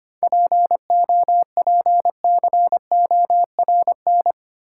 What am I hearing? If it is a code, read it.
Morse code, heard as a keyed tone, about 25 wpm, 690 Hz. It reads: POPCORN
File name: practice / 25wpm / Easy Words with Hints - 25wpm